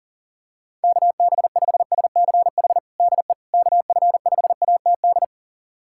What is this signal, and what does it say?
Morse code, heard as a keyed tone, about 40 wpm, 700 Hz. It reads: KB5SCH DE KF5ATD
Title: USA Callsign DE Callsign - 6 Characters - 40wpm